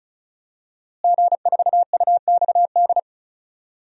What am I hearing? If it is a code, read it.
Morse code, heard as a keyed tone, about 36 wpm, 700 Hz. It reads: G4UXD
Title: DXCC-100 Call Signs - 5 Characters - Repeated 2x - 35wpm